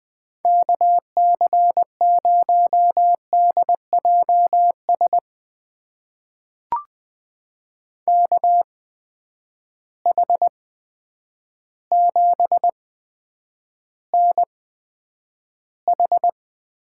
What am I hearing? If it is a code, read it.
Morse code, heard as a keyed tone, about 20 wpm, spaced as for 9 wpm, 700 Hz. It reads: KC0DJS KH7NH
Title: USA Call Signs - 8x Character Spacing - 20wpm